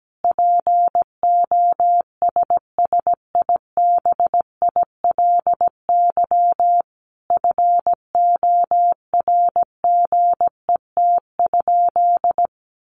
Morse code, heard as a keyed tone, about 17 wpm, 700 Hz. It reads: POSSIBILY FORGET?
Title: Sentences from Top 1300 Words 17wpm